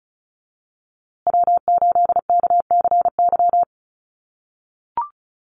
Morse code, heard as a keyed tone, about 34 wpm, 700 Hz. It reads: W8KCY
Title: USA Call Signs Repeated 2x 35wpm